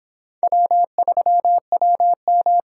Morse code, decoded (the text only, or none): W3WM